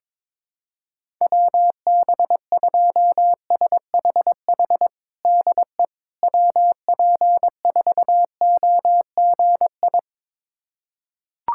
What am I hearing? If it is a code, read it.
Morse code, heard as a keyed tone, about 22 wpm, 700 Hz. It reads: WB2SHH DE WP4OGI
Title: USA Callsign DE Callsign - 6 Characters - 22wpm